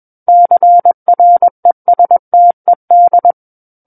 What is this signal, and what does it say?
Morse code, heard as a keyed tone, about 21 wpm, 700 Hz. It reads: CRESTED
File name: Book One / 12-Chapter 12 - 21 WPM